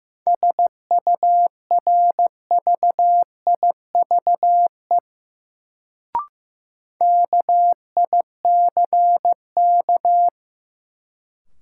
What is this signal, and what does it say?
Morse code, heard as a keyed tone, about 15 wpm, 700 Hz. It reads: SURVIVE KICK
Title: Sentences from Top 1300 Words - New Word Drill - 15wpm